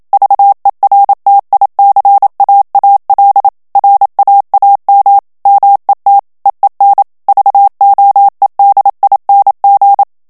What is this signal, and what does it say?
Morse code, heard as a keyed tone, about 27 wpm, 800 Hz. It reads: VERTICAAL RAAM MET EEN VOEDING
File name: HOLLANDAIS texte clair 1 vitesse 26 mots minute